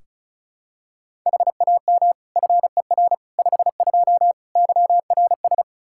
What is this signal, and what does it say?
Morse code, heard as a keyed tone, about 36 wpm, 700 Hz. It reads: HAM FER 52 YRS
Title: Practice QSO - Exchange 3 - More Info - 35wpm